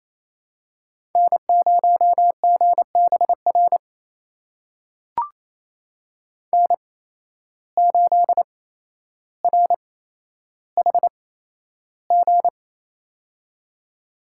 Morse code, heard as a keyed tone, about 27 wpm, spaced as for 12 wpm, 700 Hz. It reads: N0GBR N8RHG
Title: USA Call Signs - 5 Characters - 8x Character Spacing - 28wpm